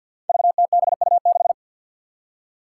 Morse code, heard as a keyed tone, about 53 wpm, 700 Hz. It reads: VTBU6
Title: Five Challenging Letter-Number Combinations 50wpm